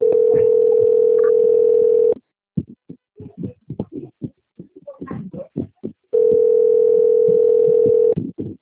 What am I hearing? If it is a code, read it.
Ringback tone (North America), 440 + 480 Hz.
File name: out- - - - - .29763